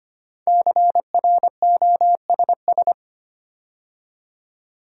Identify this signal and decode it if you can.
Morse code, heard as a keyed tone, about 25 wpm, 700 Hz. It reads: CROSS
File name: 5 Letter Words - First 3 Letters Given - 25wpm